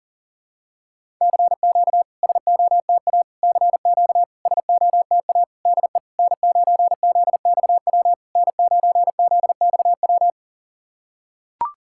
Morse code, heard as a keyed tone, about 40 wpm, 690 Hz. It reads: CQ SOTA CQ SOTA DE N9ZXW N9ZXW
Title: SOTA QSO Practice 40wpm